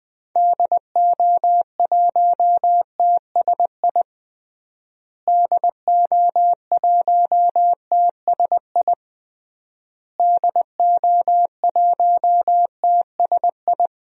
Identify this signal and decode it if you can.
Morse code, heard as a keyed tone, about 20 wpm, 700 Hz. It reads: DO1TSI DO1TSI DO1TSI
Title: DXCC-100 Call Signs - 6 Characters - Repeated 3x - 20wpm